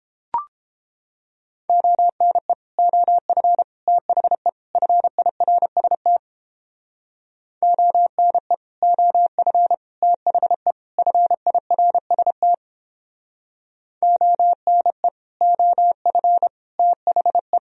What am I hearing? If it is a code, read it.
Morse code, heard as a keyed tone, about 31 wpm, 700 Hz. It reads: ONE OF THE FIRST ONE OF THE FIRST ONE OF THE